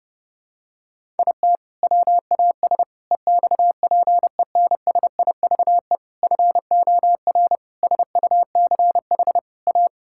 Morse code, heard as a keyed tone, about 30 wpm, 700 Hz. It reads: IT WAS EXPENSIVE FOR SUCH A
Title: Sentences from Top 1000 Words - Review Part 6 - 30wpm